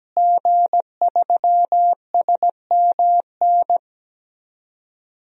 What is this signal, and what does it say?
Morse code, heard as a keyed tone, about 17 wpm, 700 Hz. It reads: G3SMN
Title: DXCC-100 Call Signs - 5 Characters - Repeated 3x - 17wpm